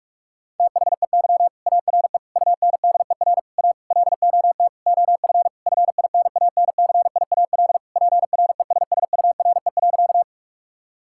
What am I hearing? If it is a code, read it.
Morse code, heard as a keyed tone, about 46 wpm, 700 Hz. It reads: THEY ARE UNDER A LOT OF FINANCIAL PRESSURE.
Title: Sentences from Top 1000 Words - Review Part 6 - 45wpm